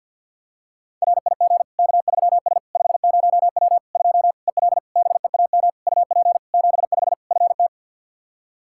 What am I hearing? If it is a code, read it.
Morse code, heard as a keyed tone, about 52 wpm, 700 Hz. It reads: RIG K3S 50W 2 EL BEAM UP 75 FT
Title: Practice QSO - Exchange 2 - Info - 50wpm